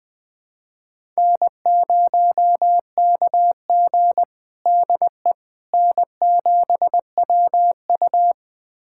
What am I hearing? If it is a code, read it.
Morse code, heard as a keyed tone, about 20 wpm, 700 Hz. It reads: N0KG DE N7WU
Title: USA Callsign DE Callsign - 4 Characters - 20wpm